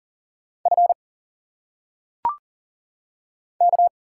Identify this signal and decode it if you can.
Morse code, heard as a keyed tone, about 40 wpm, 700 Hz. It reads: F K
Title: Single Letters - Phonetic Pronunciation 40wpm